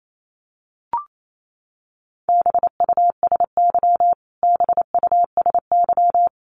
Morse code, heard as a keyed tone, about 28 wpm, 700 Hz. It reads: BUSY BUSY
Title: Top 500 Words Repeated 28wpm